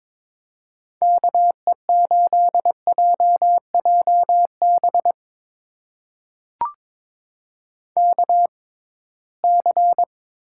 Morse code, heard as a keyed tone, about 22 wpm, spaced as for 17 wpm, 700 Hz. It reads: KE8JJB KC
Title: USA Call Signs - 6 Characters - 6x Character Spacing - 22wpm